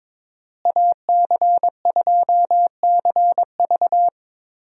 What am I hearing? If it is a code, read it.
Morse code, heard as a keyed tone, about 21 wpm, 700 Hz. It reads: AC2CV
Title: USA Call Signs - 5 Characters - Repeated 3x - 22wpm